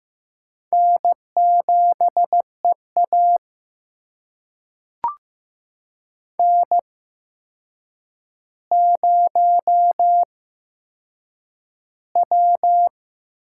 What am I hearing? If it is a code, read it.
Morse code, heard as a keyed tone, about 15 wpm, spaced as for 7 wpm, 700 Hz. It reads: N7EA N0W
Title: USA Call Signs - 4 Characters - 8x Character Spacing - 15wpm